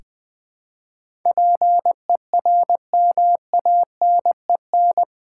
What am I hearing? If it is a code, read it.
Morse code, heard as a keyed tone, about 20 wpm, 700 Hz. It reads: PERMANEN
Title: Build A Word - 4 Letters At A Time - 11-Letter Words - 20wpm